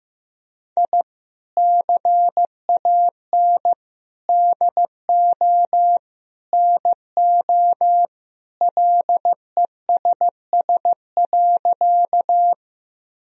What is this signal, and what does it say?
Morse code, heard as a keyed tone, about 16 wpm, 700 Hz. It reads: I CAN DO NO LESS.
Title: Sentences from Top 1000 Words - Review Part 2 - 15wpm